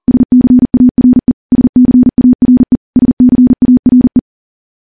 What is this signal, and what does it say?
Morse code, heard as a keyed tone, about 40 wpm, 250 Hz. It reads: SCARE SCARE SCARE
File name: morse / morse-training-40WPM